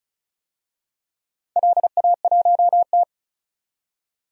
Morse code, heard as a keyed tone, about 36 wpm, 700 Hz. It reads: LA1T
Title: DXCC-100 Call Signs - 4 Characters - 35wpm